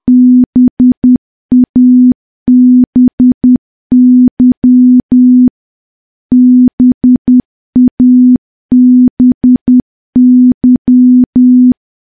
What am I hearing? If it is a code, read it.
Morse code, heard as a keyed tone, about 10 wpm, 250 Hz. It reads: BABY BABY
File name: morse / morse-training-10WPM